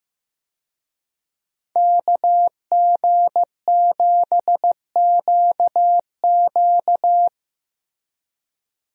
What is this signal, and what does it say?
Morse code, heard as a keyed tone, about 15 wpm, 700 Hz. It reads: KG7QQ
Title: USA Call Signs - 5 Characters - Repeated 2x - 15wpm